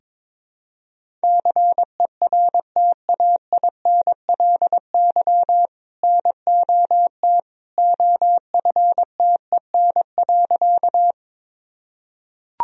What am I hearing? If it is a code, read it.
Morse code, heard as a keyed tone, about 21 wpm, 700 Hz. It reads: CERTAINLY NOT OFTEN.
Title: Sentences from Top 1000 Words - Review Part 1 - 22wpm